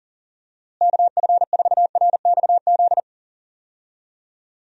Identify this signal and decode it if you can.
Morse code, heard as a keyed tone, about 40 wpm, 700 Hz. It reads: KF4RXZ
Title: USA Call Signs - 6 Characters - Repeated 3x - 40wpm